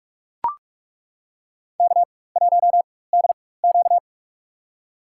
Morse code, heard as a keyed tone, about 42 wpm, spaced as for 20 wpm, 700 Hz. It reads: K1DQ
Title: USA Call Signs - 4 Characters - 4x Character Spacing - 45wpm